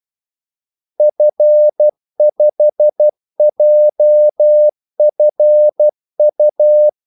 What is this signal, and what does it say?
Morse code, heard as a keyed tone, about 12 wpm, 600 Hz. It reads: F5JFU
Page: 2 ekstra ordmellemrum mellem kaldesignalerne